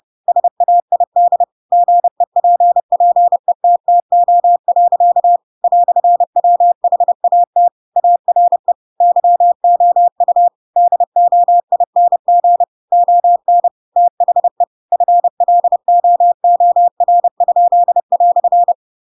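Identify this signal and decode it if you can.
Morse code, heard as a keyed tone, about 30 wpm, 700 Hz. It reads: SAID GEPPETTO. "WHAT ARE YOU DOING ON THE FLOOR?"
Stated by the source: Chapter 2 of The Adventures of Pinocchio read as a book in Morse code (CW) at 30 wpm, by ik1hge.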